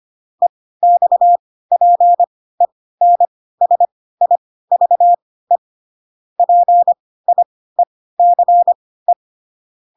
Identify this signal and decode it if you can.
Morse code, heard as a keyed tone, about 25 wpm, spaced as for 16 wpm, 700 Hz. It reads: EXPENSIVE PIECE
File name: Pinocchio / Pinocchio ch1 10 25 ik1hge